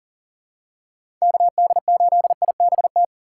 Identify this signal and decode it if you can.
Morse code, heard as a keyed tone, about 37 wpm, 700 Hz. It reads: KD8IBT
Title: USA Call Signs - 6 Characters - 3x Character Spacing - 40wpm